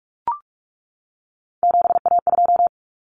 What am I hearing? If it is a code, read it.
Morse code, heard as a keyed tone, about 44 wpm, 700 Hz. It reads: 7A2
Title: Common Cut Numbers - 3-Digit 45wpm